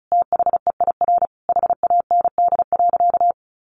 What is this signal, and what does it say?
Morse code, heard as a keyed tone, about 34 wpm, 700 Hz. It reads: THEIR HAND.
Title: Sentences from Top 1300 Words 35wpm